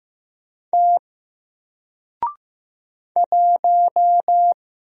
Morse code, heard as a keyed tone, about 15 wpm, 700 Hz. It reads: T 1
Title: Single Letter-Number 15wpm